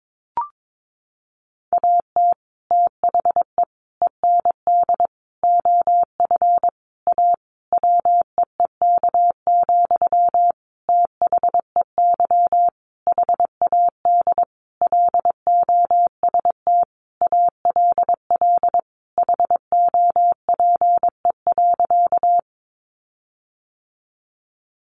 Morse code, heard as a keyed tone, about 22 wpm, 700 Hz. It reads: AT THE END OF A WEEK, THEY HAD LOST ALL HOPE.
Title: Sentences from Top 1400 Words 22wpm